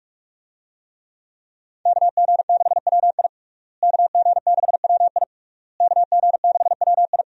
Morse code, heard as a keyed tone, about 46 wpm, 700 Hz. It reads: KG6WI KG6WI KG6WI
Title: USA Call Signs - 5 Characters - Repeated 3x - 45wpm